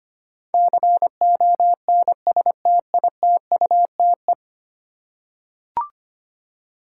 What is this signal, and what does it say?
Morse code, heard as a keyed tone, about 25 wpm, 700 Hz. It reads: CONSTITUTE
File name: 10 Letter Words - First 3 Letters Given - 25wpm